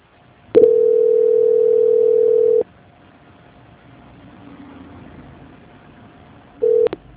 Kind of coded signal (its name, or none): telephone call progress tone